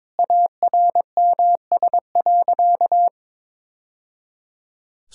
Morse code, heard as a keyed tone, about 22 wpm, 700 Hz. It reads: ARMS.
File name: practice / 22wpm / Easy Sentences with Hints - 22wpm